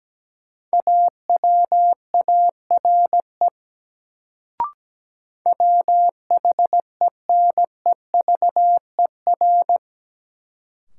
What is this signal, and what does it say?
Morse code, heard as a keyed tone, about 17 wpm, 700 Hz. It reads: AWARE WHENEVER
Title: Sentences from Top 1200 Words - New Word Drill - 17wpm